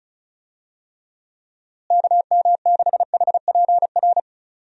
Morse code, heard as a keyed tone, about 36 wpm, 700 Hz. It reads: KM6HPR
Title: USA Call Signs - 6 Characters - Repeated 3x - 35wpm